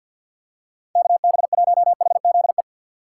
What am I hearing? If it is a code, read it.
Morse code, heard as a keyed tone, about 50 wpm, 700 Hz. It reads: KB1HZE
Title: USA Call Signs - 6 Characters - 6x Character Spacing - 50wpm